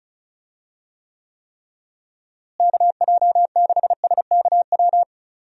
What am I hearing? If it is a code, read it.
Morse code, heard as a keyed tone, about 36 wpm, 700 Hz. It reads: KJ6SKW
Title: USA Call Signs - 6 Characters - Repeated 3x - 35wpm